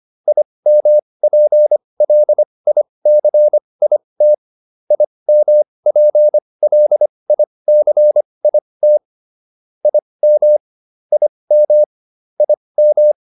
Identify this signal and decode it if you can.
Morse code, heard as a keyed tone, about 25 wpm, spaced as for 20 wpm, 600 Hz. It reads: IMPLICIT IMPLICIT IM IM IM